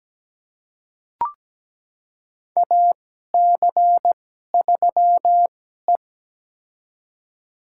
Morse code, heard as a keyed tone, about 17 wpm, spaced as for 12 wpm, 700 Hz. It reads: AC3E
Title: USA Call Signs - 4 Characters - 2x Character Spacing - 17wpm